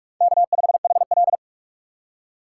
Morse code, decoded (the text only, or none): K5HL